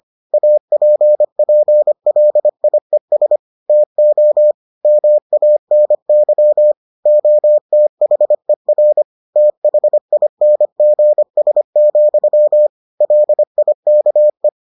Morse code, heard as a keyed tone, about 25 wpm, 600 Hz. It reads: APPLIES TO MANY OTHER THINGS, LIKE